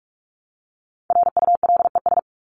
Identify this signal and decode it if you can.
Morse code, heard as a keyed tone, about 44 wpm, 700 Hz. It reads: RULES
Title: ES:S Suffix - Top 100 Words with Base Words 45wpm